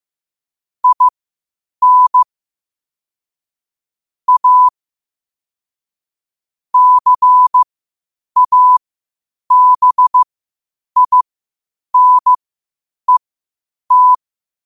Morse code, heard as a keyed tone, about 15 wpm, spaced as for 8 wpm, 1000 Hz. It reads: IN A CABINET
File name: qotdfiles / 8wpm